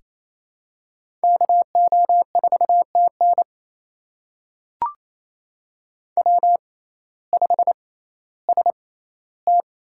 Morse code, heard as a keyed tone, about 28 wpm, spaced as for 15 wpm, 700 Hz. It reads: KO4TN W5ST
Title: USA Call Signs - 5 Characters - 6x Character Spacing - 28wpm